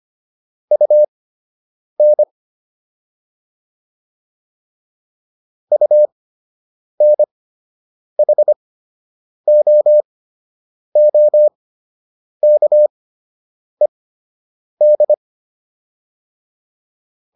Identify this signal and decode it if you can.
Morse code, heard as a keyed tone, about 25 wpm, spaced as for 8 wpm, 600 Hz. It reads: UN UNHOOKED